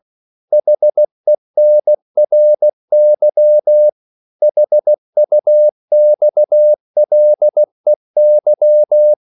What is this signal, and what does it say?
Morse code, heard as a keyed tone, about 16 wpm, 600 Hz. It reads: HENRY HUXLEY
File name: CW / Practice / EnglishQuotations0116